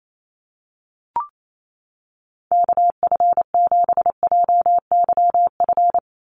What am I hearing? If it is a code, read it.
Morse code, heard as a keyed tone, about 28 wpm, 700 Hz. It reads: KF7JYF
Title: USA Call Signs 28wpm